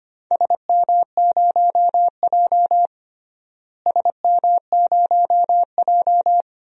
Morse code, heard as a keyed tone, about 25 wpm, 700 Hz. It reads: SM0J SM0J
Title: DXCC-100 Call Signs - 4 Characters - Repeated 3x - 25wpm